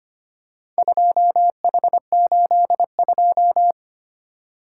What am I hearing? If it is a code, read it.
Morse code, heard as a keyed tone, about 25 wpm, 700 Hz. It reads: 2H82
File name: practice / 25wpm / Four Letter-Number Combinations - 25wpm